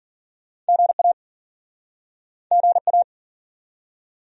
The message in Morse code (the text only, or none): GA GA